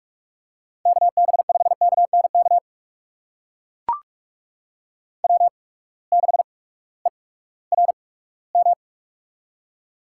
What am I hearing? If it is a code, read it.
Morse code, heard as a keyed tone, about 44 wpm, spaced as for 20 wpm, 700 Hz. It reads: KB5KNK W6ERM
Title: USA Call Signs - 8x Character Spacing - 45wpm